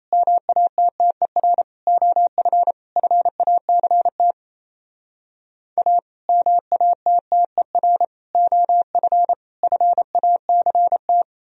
Morse code, heard as a keyed tone, about 35 wpm, 700 Hz. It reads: MATTER OF FACT A MATTER OF FACT